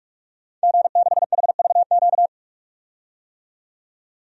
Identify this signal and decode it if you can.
Morse code, heard as a keyed tone, about 45 wpm, 690 Hz. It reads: G6HVQ
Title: DXCC-100 Call Signs - 5 Characters - 45wpm